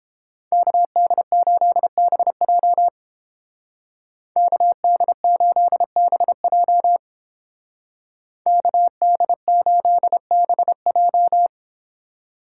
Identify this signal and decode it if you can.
Morse code, heard as a keyed tone, about 28 wpm, 700 Hz. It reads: KD8BJ KD8BJ KD8BJ